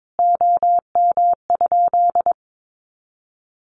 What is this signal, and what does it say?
Morse code, heard as a keyed tone, about 22 wpm, 700 Hz. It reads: OM?
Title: Common QSO Phrases 22wpm